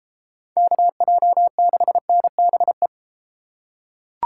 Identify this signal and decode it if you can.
Morse code, heard as a keyed tone, about 30 wpm, 700 Hz. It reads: KJ6NBE